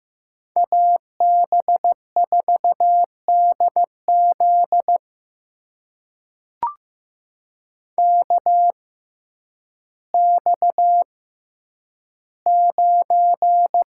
Morse code, heard as a keyed tone, about 15 wpm, spaced as for 9 wpm, 700 Hz. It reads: AB4DZ KX9
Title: USA Call Signs - 6x Character Spacing - 15wpm